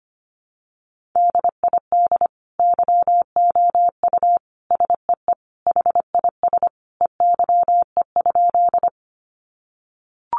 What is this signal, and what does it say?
Morse code, heard as a keyed tone, about 25 wpm, 700 Hz. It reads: DID YOU SEE HIS EYE?
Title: Sentences from Top 300 Words 25wpm